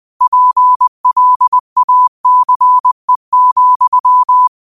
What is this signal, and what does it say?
Morse code, heard as a keyed tone, about 20 wpm, 1000 Hz. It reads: PLACE,